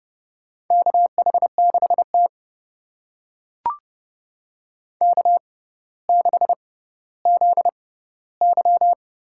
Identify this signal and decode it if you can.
Morse code, heard as a keyed tone, about 32 wpm, spaced as for 16 wpm, 700 Hz. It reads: KH6T K6ZY
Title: USA Call Signs - 4 Characters - 6x Character Spacing - 30wpm